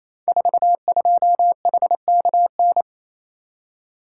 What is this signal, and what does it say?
Morse code, heard as a keyed tone, about 29 wpm, 700 Hz. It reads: 42HKN
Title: Code Groups - 5 Letter:Numbers - 28wpm